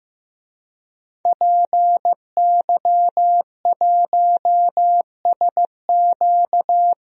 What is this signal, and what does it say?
Morse code, heard as a keyed tone, about 15 wpm, 700 Hz. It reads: PY1SQ
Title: DXCC-100 Call Signs - 5 Characters - 15wpm